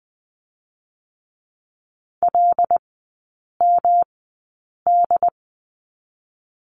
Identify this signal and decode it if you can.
Morse code, heard as a keyed tone, about 20 wpm, spaced as for 8 wpm, 700 Hz. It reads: LMD
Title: 166 Lesson 27e - Number 7 - Three Characters 20wpm